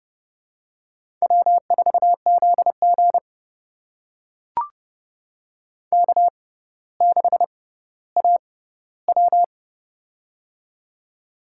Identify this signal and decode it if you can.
Morse code, heard as a keyed tone, about 30 wpm, spaced as for 15 wpm, 700 Hz. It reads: W4ZG K6AW
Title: USA Call Signs - 4 Characters - 6x Character Spacing - 30wpm